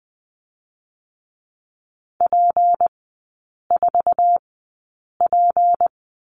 Morse code, heard as a keyed tone, about 20 wpm, 700 Hz. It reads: P 4 P
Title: 198 Lesson 32e - Letter k - Three Characters 20wpm